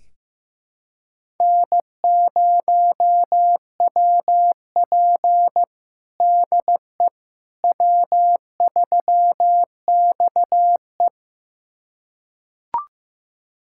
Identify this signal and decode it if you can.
Morse code, heard as a keyed tone, about 15 wpm, 700 Hz. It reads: N0WP DE W3XE
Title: USA Callsign DE Callsign - 4 Characters - 15wpm